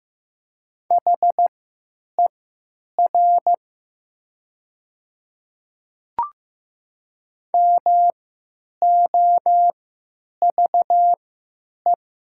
Morse code, heard as a keyed tone, about 15 wpm, spaced as for 9 wpm, 700 Hz. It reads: HER MOVE
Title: 3x Character Spacing - Top 200 Words - 15wpm